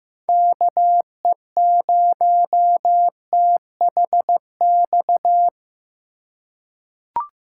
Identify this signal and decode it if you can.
Morse code, heard as a keyed tone, about 16 wpm, 700 Hz. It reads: KE0THX
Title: USA Call Signs - 6 Characters - 2x Character Spacing - 15wpm